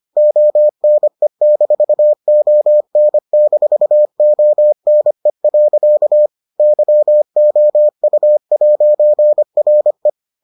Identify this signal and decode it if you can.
Morse code, heard as a keyed tone, about 25 wpm, 600 Hz. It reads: ONE-ON-ONE. YOU'RE